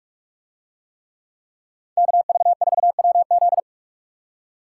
Morse code, heard as a keyed tone, about 46 wpm, 700 Hz. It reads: KV4WZ